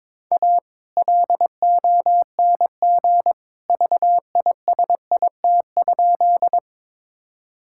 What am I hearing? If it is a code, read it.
Morse code, heard as a keyed tone, about 22 wpm, 700 Hz. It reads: A LONG VISIT?
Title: Sentences from Top 1000 Words - Review Part 6 - 22wpm